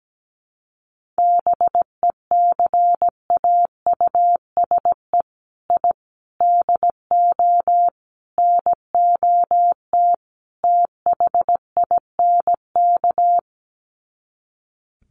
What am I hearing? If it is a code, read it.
Morse code, heard as a keyed tone, about 17 wpm, 700 Hz. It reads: BECAUSE I DO NOT THINK
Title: Sets of 5 Words Encore 17wpm